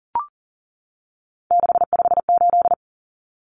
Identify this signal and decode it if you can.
Morse code, heard as a keyed tone, about 40 wpm, 700 Hz. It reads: 658
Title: Numbers - 3-Digit 40wpm